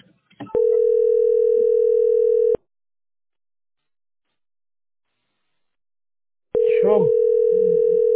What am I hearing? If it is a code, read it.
Ringback tone (North America), 440 + 480 Hz.